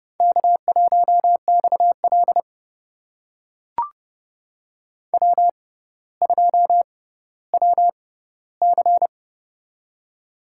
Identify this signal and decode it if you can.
Morse code, heard as a keyed tone, about 31 wpm, spaced as for 16 wpm, 700 Hz. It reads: K1XL W2WC
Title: USA Call Signs - 4 Characters - 6x Character Spacing - 30wpm